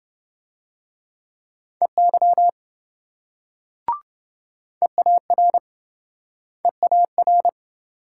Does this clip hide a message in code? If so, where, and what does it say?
Yes: Morse code from 1.8 to 7.5 s, heard as a keyed tone, about 27 wpm, 700 Hz. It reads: EY EAR EAR